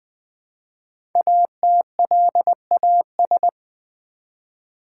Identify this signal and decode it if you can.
Morse code, heard as a keyed tone, about 20 wpm, 700 Hz. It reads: ATLAS
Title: Easy Words with Hints - 20wpm